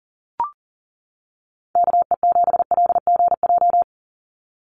Morse code, heard as a keyed tone, about 40 wpm, 700 Hz. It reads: KE7LGJ
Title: USA Call Signs 40wpm